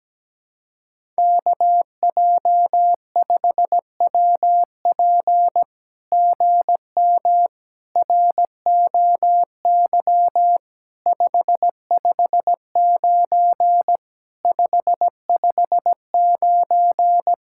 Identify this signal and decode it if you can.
Morse code, heard as a keyed tone, about 17 wpm, 700 Hz. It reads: KJ5WP GM ROY 559 559
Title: SOTA QSO Practice 17wpm